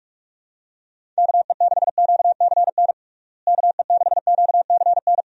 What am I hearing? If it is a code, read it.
Morse code, heard as a keyed tone, about 41 wpm, 700 Hz. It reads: KE6QCN KE6QCN
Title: USA Call Signs - 6 Characters - Repeated 2x - 45wpm